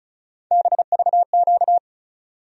Morse code, heard as a keyed tone, about 36 wpm, 700 Hz. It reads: BVQ
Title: USA Call Sign Postfixes - 3 Letters 35wpm